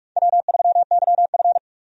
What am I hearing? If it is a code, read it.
Morse code, heard as a keyed tone, about 46 wpm, 700 Hz. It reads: W3YF
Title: USA Call Signs - 4 Characters - 45wpm